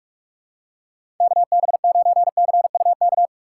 Morse code, heard as a keyed tone, about 46 wpm, 700 Hz. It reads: KB9CUK
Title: USA Call Signs - 6 Characters - 6x Character Spacing - 45wpm